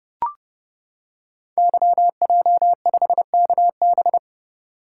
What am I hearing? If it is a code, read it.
Morse code, heard as a keyed tone, about 30 wpm, 700 Hz. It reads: YJ5KB
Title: Code Groups - 5 Letter:Numbers - 30wpm